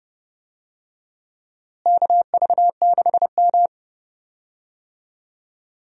Morse code, heard as a keyed tone, about 32 wpm, 700 Hz. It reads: KV6M
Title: USA Call Signs - 4 Characters - Repeated 2x - 30wpm